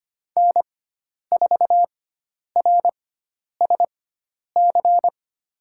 Morse code, heard as a keyed tone, about 25 wpm, spaced as for 10 wpm, 700 Hz. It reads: N4RSC
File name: course / 25:10wpm / 095 Lesson 16g - Letter c - Call Signs 25wpm